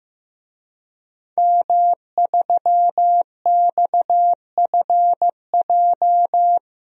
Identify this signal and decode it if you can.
Morse code, heard as a keyed tone, about 15 wpm, 700 Hz. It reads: M3XFJ
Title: DXCC-100 Call Signs - 5 Characters - Repeated 2x - 15wpm